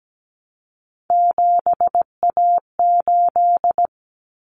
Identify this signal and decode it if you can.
Morse code, heard as a keyed tone, about 17 wpm, 700 Hz. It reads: 7A8
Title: Common Cut Numbers - 3-Digit 17wpm